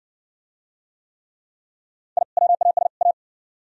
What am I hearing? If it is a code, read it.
Morse code, heard as a keyed tone, about 61 wpm, 700 Hz. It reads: I WAS A